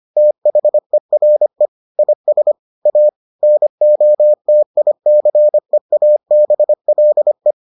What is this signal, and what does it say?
Morse code, heard as a keyed tone, about 25 wpm, 600 Hz. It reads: THERE IS A NOTICEABLE